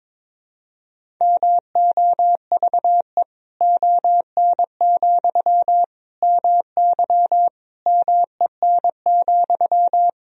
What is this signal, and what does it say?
Morse code, heard as a keyed tone, about 22 wpm, 700 Hz. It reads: MOVE ON, MY MEN,
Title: Sentences from Top 500 Words and Abbreviations 22wpm